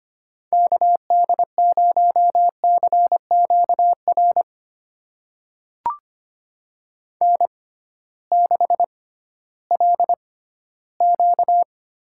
Morse code, heard as a keyed tone, about 25 wpm, spaced as for 15 wpm, 700 Hz. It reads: KD0CQR N6LQ
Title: USA Call Signs - 6x Character Spacing - 25wpm